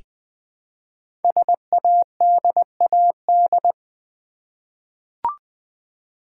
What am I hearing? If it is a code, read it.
Morse code, heard as a keyed tone, about 20 wpm, 700 Hz. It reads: SADAD
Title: Code Groups - 5 Letters - 20wpm